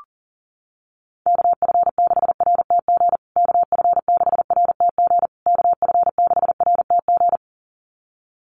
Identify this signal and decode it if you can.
Morse code, heard as a keyed tone, about 40 wpm, 700 Hz. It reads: KF6RTG KF6RTG KF6RTG
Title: USA Call Signs Repeated 3x 40wpm